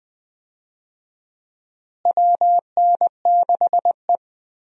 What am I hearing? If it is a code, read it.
Morse code, heard as a keyed tone, about 20 wpm, 700 Hz. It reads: WN6E
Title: USA Call Signs - 4 Characters - Repeated 2x - 20wpm